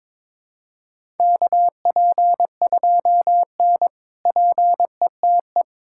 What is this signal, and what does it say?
Morse code, heard as a keyed tone, about 21 wpm, 700 Hz. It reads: KP2N PETE
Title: K1USN SST QSO Practice - 22wpm